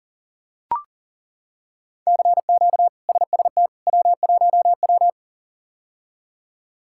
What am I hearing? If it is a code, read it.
Morse code, heard as a keyed tone, about 40 wpm, 700 Hz. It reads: CQ SST W1W
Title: K1USN SST QSO Practice - 40wpm